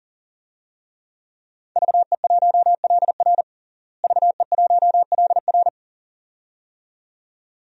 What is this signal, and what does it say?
Morse code, heard as a keyed tone, about 40 wpm, 700 Hz. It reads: VE1LR VE1LR
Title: DXCC-100 Call Signs - 5 Characters - Repeated 2x - 40wpm